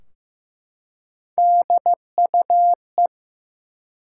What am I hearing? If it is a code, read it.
Morse code, heard as a keyed tone, about 15 wpm, 700 Hz. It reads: DUE
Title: Three Letter Words - 15wpm